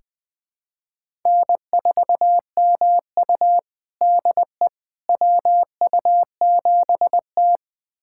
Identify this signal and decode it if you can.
Morse code, heard as a keyed tone, about 21 wpm, 700 Hz. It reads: N4MU DE WU7T
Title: USA Callsign DE Callsign - 4 Characters - 20wpm